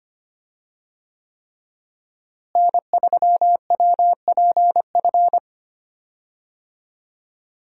Morse code, heard as a keyed tone, about 25 wpm, 700 Hz. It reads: N3WPF
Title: USA Call Signs - 5 Characters - Repeated 3x - 25wpm